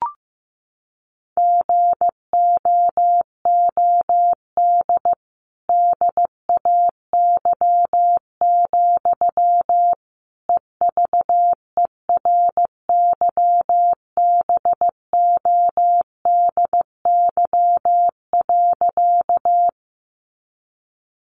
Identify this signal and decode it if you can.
Morse code, heard as a keyed tone, about 15 wpm, 700 Hz. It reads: GOOD DAY, EVERYBODY.
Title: Sentences from Top 1400 Words 15wpm